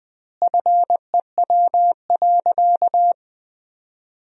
Morse code, heard as a keyed tone, about 20 wpm, 700 Hz. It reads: FEW.
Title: 2x Word-Spacing - Sentences from Top 300 Words - 20wpm